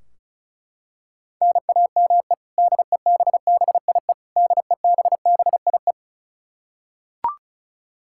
Morse code, heard as a keyed tone, about 36 wpm, 700 Hz. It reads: NAME DEBBIE DEBBIE
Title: Practice QSO - Exchange 1 - Beginning - 35wpm